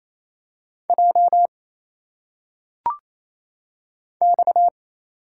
Morse code, heard as a keyed tone, about 27 wpm, 700 Hz. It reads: J X
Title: Single Letters - Phonetic Pronunciation 28wpm